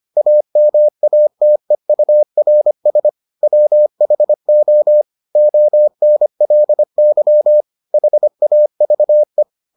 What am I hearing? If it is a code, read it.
Morse code, heard as a keyed tone, about 25 wpm, 600 Hz. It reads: AMATEURS WHO ONLY HAVE